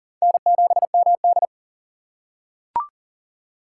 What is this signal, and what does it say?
Morse code, heard as a keyed tone, about 40 wpm, 700 Hz. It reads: N7MD
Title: USA Call Signs - 4 Characters - 6x Character Spacing - 40wpm